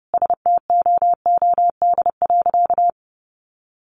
Morse code, heard as a keyed tone, about 30 wpm, 700 Hz. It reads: STOOD.
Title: Sentences from Top 500 Words 30wpm